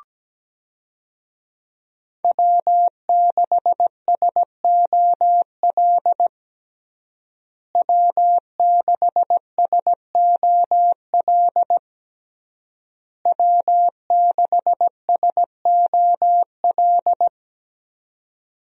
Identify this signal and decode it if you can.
Morse code, heard as a keyed tone, about 17 wpm, 700 Hz. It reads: W6SOL W6SOL W6SOL
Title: USA Call Signs - 5 Characters - Repeated 3x - 17wpm